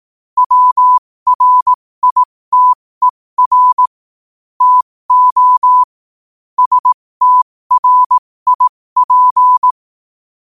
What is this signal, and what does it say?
Morse code, heard as a keyed tone, about 18 wpm, spaced as for 15 wpm, 1000 Hz. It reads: WRITER TO STRIP
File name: qotdfiles / 15wpm